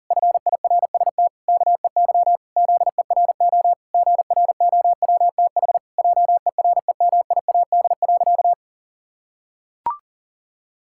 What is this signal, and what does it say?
Morse code, heard as a keyed tone, about 37 wpm, 700 Hz. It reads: FIRST KEY ZERO GROWTH JEREMIAD.
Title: Pangram Challenge v2 - 40wpm